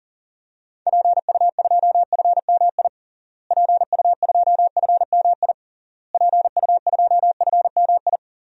Morse code, heard as a keyed tone, about 40 wpm, 700 Hz. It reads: PU2FMI PU2FMI PU2FMI
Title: DXCC-100 Call Signs - 6 Characters - Repeated 3x - 40wpm